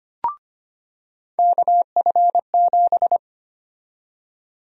Morse code, heard as a keyed tone, about 25 wpm, 700 Hz. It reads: KF7
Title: USA Call Sign Prefixes - 25wpm